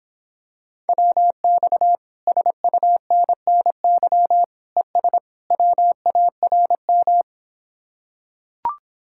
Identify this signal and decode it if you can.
Morse code, heard as a keyed tone, about 26 wpm, 700 Hz. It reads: WX SUNNY ES WARM